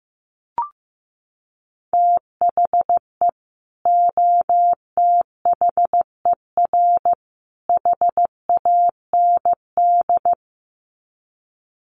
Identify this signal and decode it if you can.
Morse code, heard as a keyed tone, about 15 wpm, 700 Hz. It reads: THE OTHER HAND
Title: Sets of 3 Words 15wpm